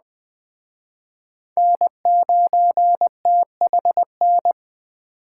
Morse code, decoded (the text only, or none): N9THN